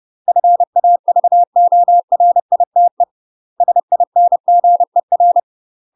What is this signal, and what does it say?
Morse code, heard as a keyed tone, about 30 wpm, 700 Hz. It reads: FAVORITE SINGER
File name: UK MASTER QSOs TRAINING / 30WPM / UK MASTER QSOs 0041